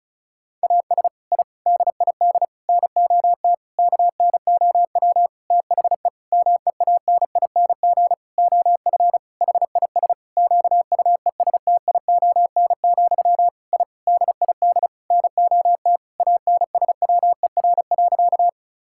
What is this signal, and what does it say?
Morse code, heard as a keyed tone, about 36 wpm, 700 Hz. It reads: AS I DID NOT KNOW THE MEANING OF HIS QUESTION, I DID NOT ANSWER.
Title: Sentences from Top 1000 Words - Review Part 6 - 35wpm